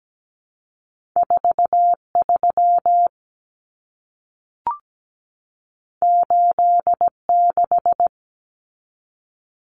Morse code, heard as a keyed tone, about 17 wpm, 700 Hz. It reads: 43 86
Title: Common Cut Numbers - 2-Digit 17wpm